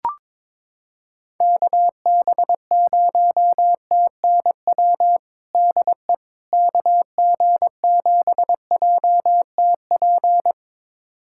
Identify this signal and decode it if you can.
Morse code, heard as a keyed tone, about 22 wpm, 700 Hz. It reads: KB0TNW DE KG7JTP
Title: USA Callsign DE Callsign - 6 Characters - 22wpm